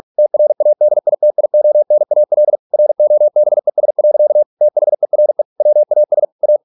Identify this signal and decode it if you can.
Morse code, heard as a keyed tone, about 46 wpm, 600 Hz. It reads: TRADITIONAL ROBES. THERE WAS A